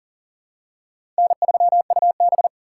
Morse code, heard as a keyed tone, about 40 wpm, 700 Hz. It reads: N3UB
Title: USA Call Signs - 4 Characters - 4x Character Spacing - 40wpm